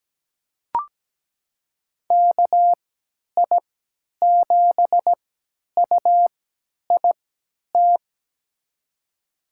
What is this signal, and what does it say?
Morse code, heard as a keyed tone, about 17 wpm, spaced as for 10 wpm, 700 Hz. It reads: KI7UIT
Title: USA Call Signs - 6 Characters - 3x Character Spacing - 17wpm